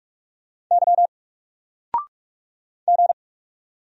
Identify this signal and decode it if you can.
Morse code, heard as a keyed tone, about 44 wpm, 700 Hz. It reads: Y G
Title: Single Letter-Number Alt 45wpm